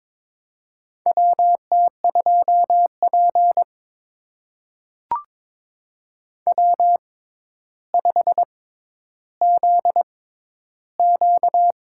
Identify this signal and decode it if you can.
Morse code, heard as a keyed tone, about 21 wpm, spaced as for 11 wpm, 700 Hz. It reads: WT2P W5ZQ
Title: USA Call Signs - 4 Characters - 6x Character Spacing - 22wpm